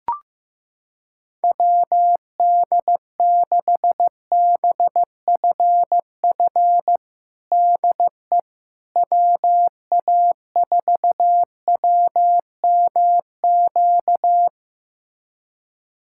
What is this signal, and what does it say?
Morse code, heard as a keyed tone, about 15 wpm, 700 Hz. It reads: WD6BFF DE WA4WMQ
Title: USA Callsign DE Callsign - 6 Characters - 15wpm